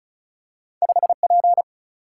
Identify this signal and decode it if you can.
Morse code, heard as a keyed tone, about 33 wpm, 700 Hz. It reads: 5 P
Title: Single-Letter Single-Number 35wpm